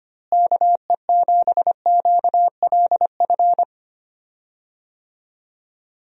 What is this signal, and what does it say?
Morse code, heard as a keyed tone, about 25 wpm, 700 Hz. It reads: KE7QLF
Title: USA Call Signs - 6 Characters - Repeated 2x - 25wpm